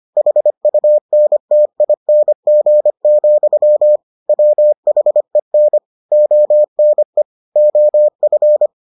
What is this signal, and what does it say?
Morse code, heard as a keyed tone, about 25 wpm, 600 Hz. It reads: HUNTING, WHEN ONE OF